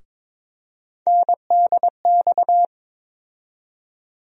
Morse code, heard as a keyed tone, about 22 wpm, 700 Hz. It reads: NDX
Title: USA Call Sign Postfixes - 3 Letters 22wpm